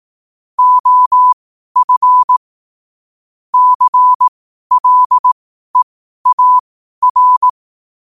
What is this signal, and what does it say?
Morse code, heard as a keyed tone, about 18 wpm, spaced as for 12 wpm, 1000 Hz. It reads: OF CLEAR
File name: qotdfiles / 12wpm